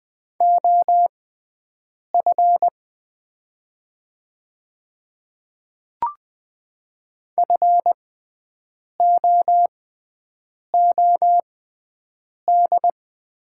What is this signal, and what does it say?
Morse code, heard as a keyed tone, about 20 wpm, spaced as for 7 wpm, 700 Hz. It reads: OF FOOD
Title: 6x Character Spacing - Top 300 Words - 20wpm